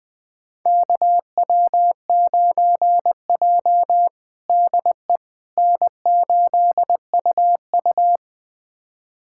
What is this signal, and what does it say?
Morse code, heard as a keyed tone, about 20 wpm, 700 Hz. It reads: KW9J DE N8UU
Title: USA Callsign DE Callsign - 4 Characters - 20wpm